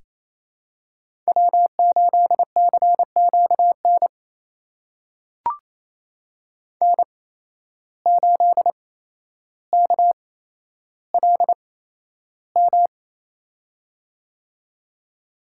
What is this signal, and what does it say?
Morse code, heard as a keyed tone, about 28 wpm, spaced as for 12 wpm, 700 Hz. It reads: W8CQN N8KLM
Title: USA Call Signs - 5 Characters - 8x Character Spacing - 28wpm